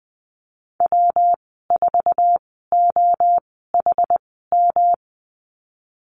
Morse code, heard as a keyed tone, about 20 wpm, spaced as for 14 wpm, 700 Hz. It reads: W4OHM
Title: 122 Lesson 20g - Number 3 - Call Signs 20wpm